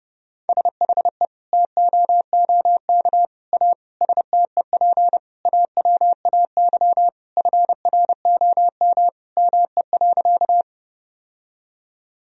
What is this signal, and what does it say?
Morse code, heard as a keyed tone, about 30 wpm, 700 Hz. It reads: SHE TOOK A STEP AWAY FROM ME.
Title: Sentences from Top 1000 Words - Review Part 2 - 30wpm